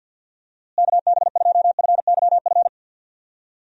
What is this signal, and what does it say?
Morse code, heard as a keyed tone, about 52 wpm, 700 Hz. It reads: KB2FYF
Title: USA Call Signs - 6 Characters - Repeated 3x - 50wpm